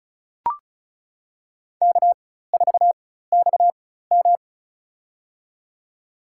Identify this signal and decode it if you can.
Morse code, heard as a keyed tone, about 36 wpm, spaced as for 16 wpm, 700 Hz. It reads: K4XM
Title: USA Call Signs - 4x Character Spacing - 35wpm